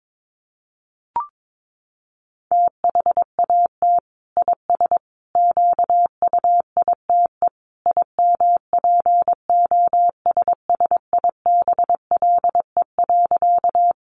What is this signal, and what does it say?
Morse code, heard as a keyed tone, about 22 wpm, 700 Hz. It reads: THAT IS QUITE IMPOSSIBLE.
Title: Sentences from Top 1100 Words 22wpm